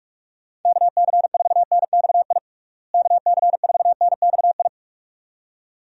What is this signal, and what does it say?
Morse code, heard as a keyed tone, about 46 wpm, 700 Hz. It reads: KC4NXI KC4NXI
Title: USA Call Signs - 6 Characters - Repeated 2x - 45wpm